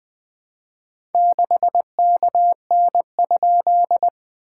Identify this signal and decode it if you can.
Morse code, heard as a keyed tone, about 20 wpm, 690 Hz. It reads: 6KN?
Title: SOTA QSO Practice 20wpm